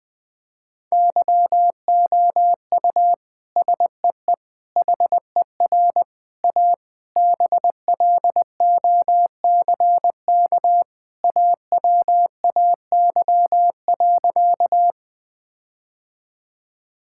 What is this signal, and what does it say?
Morse code, heard as a keyed tone, about 20 wpm, 700 Hz. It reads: YOU SEE HER A BLOCK AWAY.
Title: Sentences from Top 1600 Words 20wpm